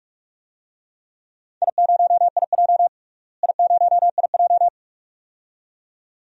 Morse code, heard as a keyed tone, about 45 wpm, 700 Hz. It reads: I0IJ I0IJ
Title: DXCC-100 Call Signs - 4 Characters - Repeated 2x - 45wpm